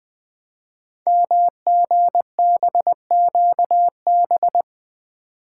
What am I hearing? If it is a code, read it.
Morse code, heard as a keyed tone, about 20 wpm, 700 Hz. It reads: MGBQB
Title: Code Groups - 5 Letters - 20wpm